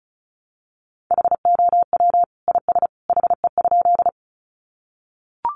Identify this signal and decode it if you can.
Morse code, heard as a keyed tone, about 34 wpm, 700 Hz. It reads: HOW IS HE?
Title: Sentences from Top 100 Words 35wpm